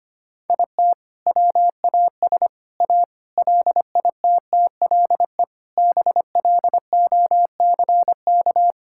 Morse code, heard as a keyed tone, about 25 wpm, 700 Hz. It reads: IT WAS A LITTLE BLOCK
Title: Sentences from Top 1600 Words 25wpm